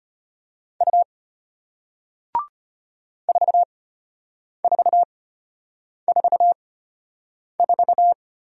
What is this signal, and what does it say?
Morse code, heard as a keyed tone, about 34 wpm, spaced as for 8 wpm, 700 Hz. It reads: U 4444